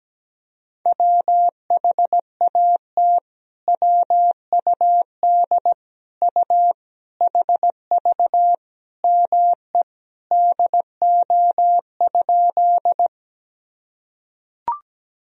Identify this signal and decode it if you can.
Morse code, heard as a keyed tone, about 17 wpm, 700 Hz. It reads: WHAT WUD U HV ME DO?
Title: Sentences from Top 200 Words and Abbreviations 17wpm